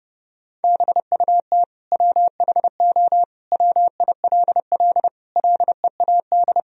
Morse code, heard as a keyed tone, about 32 wpm, 700 Hz. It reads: BUT WHO WILL LEAD
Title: Sentences from Top 1000 Words - Review Part 2 - 30wpm